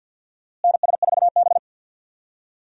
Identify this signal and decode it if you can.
Morse code, heard as a keyed tone, about 52 wpm, 700 Hz. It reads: NS4B
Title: USA Call Signs - 4 Characters - Repeated 3x - 50wpm